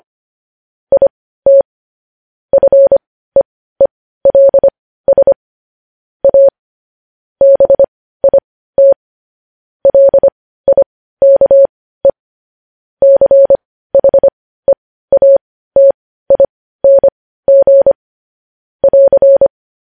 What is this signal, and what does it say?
Morse code, heard as a keyed tone, about 25 wpm, spaced as for 15 wpm, 550 Hz. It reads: IT FEELS A BIT LIKE CHEATING <AR>
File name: podcast / News headlines-1x-25x15